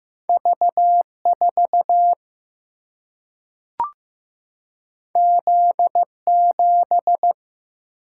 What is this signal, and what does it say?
Morse code, heard as a keyed tone, about 15 wpm, 700 Hz. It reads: V 4 Z 7
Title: Single-Letter Single-Number 15wpm